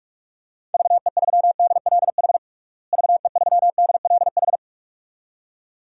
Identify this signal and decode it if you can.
Morse code, heard as a keyed tone, about 46 wpm, 700 Hz. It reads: VE3DLH VE3DLH
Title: DXCC-100 Call Signs - 6 Characters - Repeated 2x - 45wpm